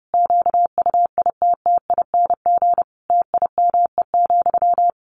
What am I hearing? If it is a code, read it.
Morse code, heard as a keyed tone, about 30 wpm, 700 Hz. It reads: QUITTING TIME,
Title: A Sound Of Thunder 30wpm